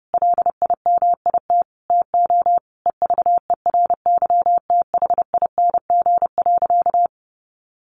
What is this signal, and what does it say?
Morse code, heard as a keyed tone, about 30 wpm, 700 Hz. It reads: LIMIT TO EVERYTHING.
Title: Sentences from Top 1500 Words 30wpm